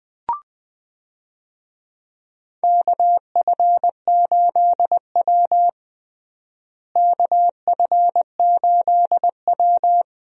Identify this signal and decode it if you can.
Morse code, heard as a keyed tone, about 20 wpm, 700 Hz. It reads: KF8W KF8W
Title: USA Call Signs - 4 Characters - Repeated 3x - 20wpm